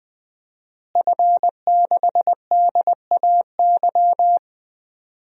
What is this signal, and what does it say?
Morse code, heard as a keyed tone, about 20 wpm, 700 Hz. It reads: F6DAY
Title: DXCC-100 Call Signs - 5 Characters - Repeated 3x - 20wpm